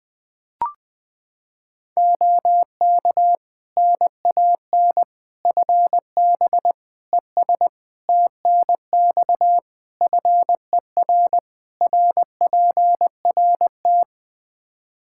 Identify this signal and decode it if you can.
Morse code, heard as a keyed tone, about 20 wpm, 700 Hz. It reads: OK NAN FB ES TNX FER RPRT
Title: Practice QSO - Exchange 2 - Info - 20wpm